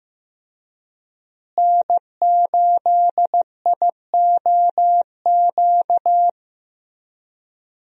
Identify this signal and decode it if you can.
Morse code, heard as a keyed tone, about 15 wpm, 700 Hz. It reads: N8IOQ
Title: USA Call Signs - 5 Characters - Repeated 2x - 15wpm